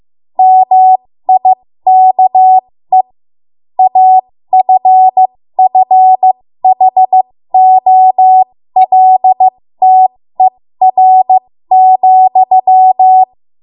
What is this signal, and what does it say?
Morse code, heard as a keyed tone, about 15 wpm, spaced as for 13 wpm, 750 Hz. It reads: MIKE AFFHOLTER,
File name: Morse / 13WPM